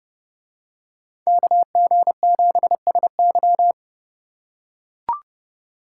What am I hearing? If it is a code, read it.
Morse code, heard as a keyed tone, about 29 wpm, 700 Hz. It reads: KG7SY